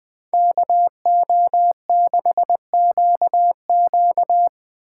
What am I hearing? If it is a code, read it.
Morse code, heard as a keyed tone, about 20 wpm, 700 Hz. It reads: KO6QQ